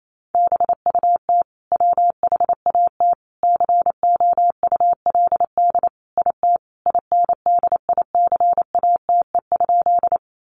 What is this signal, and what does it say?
Morse code, heard as a keyed tone, about 28 wpm, 700 Hz. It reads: BUT WHAT COULD IT INDICATE?
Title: Sentences from Top 1300 Words 28wpm